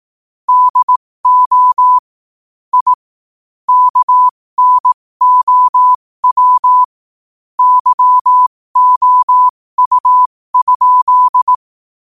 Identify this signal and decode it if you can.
Morse code, heard as a keyed tone, about 18 wpm, spaced as for 15 wpm, 1000 Hz. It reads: DO I KNOW YOU?
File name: qotdfiles / 15wpm